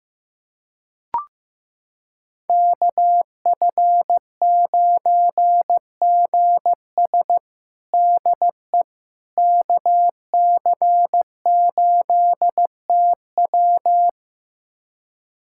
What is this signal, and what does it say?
Morse code, heard as a keyed tone, about 15 wpm, 700 Hz. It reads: KF9GS DE KC8TW
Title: USA Callsign DE Callsign - 5 Characters - 15wpm